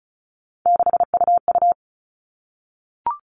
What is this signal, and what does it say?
Morse code, heard as a keyed tone, about 34 wpm, 700 Hz. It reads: 6UU
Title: All Cut Numbers - 3-Digit 35wpm